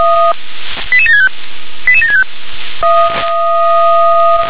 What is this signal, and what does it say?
Five-tone selective call (ZVEI), heard as quick runs of five tones, tones of about 80 ms. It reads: 88765 88765